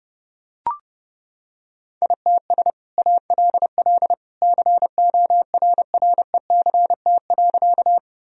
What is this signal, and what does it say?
Morse code, heard as a keyed tone, about 30 wpm, 700 Hz. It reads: ITS ALL CORRECT.
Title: Sentences from Top 1000 Words - Review Part 2 - 30wpm